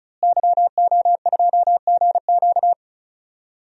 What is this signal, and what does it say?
Morse code, heard as a keyed tone, about 34 wpm, 700 Hz. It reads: YO2GQ
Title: DXCC-100 Call Signs - 5 Characters - Repeated 2x - 35wpm